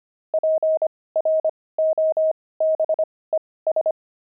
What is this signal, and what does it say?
Morse code, heard as a keyed tone, about 25 wpm, spaced as for 18 wpm, 630 Hz. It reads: PROBES
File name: pangram-fs-18-cs-25-wived